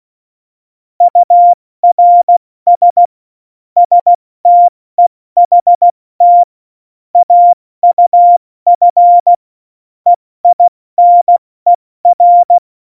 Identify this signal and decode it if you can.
Morse code, heard as a keyed tone, about 16 wpm, 700 Hz. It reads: URS STEHT AUF EINER
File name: Morsekurs / Witze / Witz 24. Oktober 17, Tempo 70